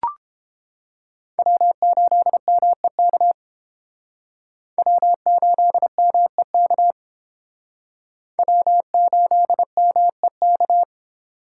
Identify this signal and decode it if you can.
Morse code, heard as a keyed tone, about 30 wpm, 700 Hz. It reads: W8MEK W8MEK W8MEK